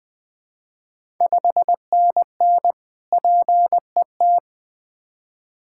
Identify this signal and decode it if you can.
Morse code, heard as a keyed tone, about 20 wpm, 700 Hz. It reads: 5NN PET
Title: Ontario QSO Party Exchange 20wpm